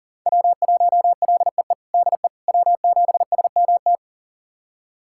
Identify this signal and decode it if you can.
Morse code, heard as a keyed tone, about 40 wpm, 700 Hz. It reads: W1LEE DE W7SMT
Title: USA Callsign DE Callsign - 5 Characters - 40wpm